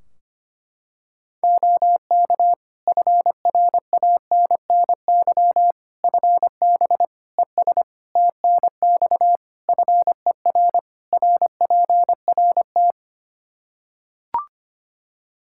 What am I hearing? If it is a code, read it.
Morse code, heard as a keyed tone, about 25 wpm, 700 Hz. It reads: OK FRANNY FB ES TNX FER RPRT
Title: Practice QSO - Exchange 2 - Info - 25wpm